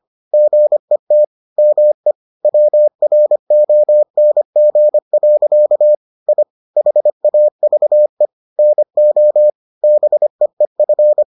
Morse code, heard as a keyed tone, about 25 wpm, 600 Hz. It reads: GET ME WRONG. I HAVE NO BEEF